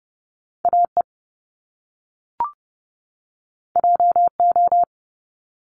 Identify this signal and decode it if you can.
Morse code, heard as a keyed tone, about 30 wpm, 700 Hz. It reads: AE JO
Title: Top Two Letter Combinations in Words 30wpm